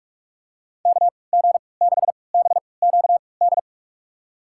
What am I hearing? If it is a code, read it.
Morse code, heard as a keyed tone, about 46 wpm, spaced as for 26 wpm, 690 Hz. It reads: KG6BQD
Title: USA Call Signs - 6 Characters - 3x Character Spacing - 45wpm